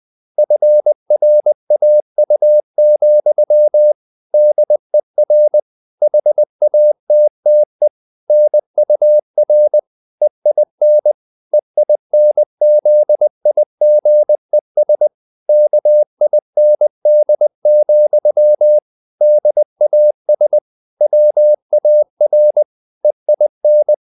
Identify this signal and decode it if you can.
Morse code, heard as a keyed tone, about 20 wpm, 600 Hz. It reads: FRAU, DER HATTE NUR EIN EINZIGES KIND, DAS WAR EIN